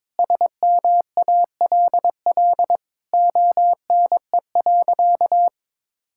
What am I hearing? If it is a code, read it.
Morse code, heard as a keyed tone, about 22 wpm, 700 Hz. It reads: SMALL ONE.
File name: practice / 22wpm / Sentences with 2-Word Phrases - 22wpm